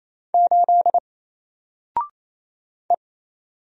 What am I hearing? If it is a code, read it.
Morse code, heard as a keyed tone, about 27 wpm, 700 Hz. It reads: 8 E
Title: Single Character (Letter:Number:Punctuation:Prosign) 28wpm